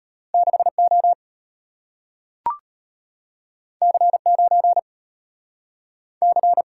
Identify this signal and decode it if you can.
Morse code, heard as a keyed tone, about 34 wpm, 700 Hz. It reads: 6O C9 C